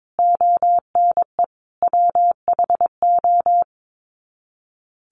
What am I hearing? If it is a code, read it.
Morse code, heard as a keyed tone, about 22 wpm, 700 Hz. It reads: ONE WHO
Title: Sets of 5 Words Encore2 22wpm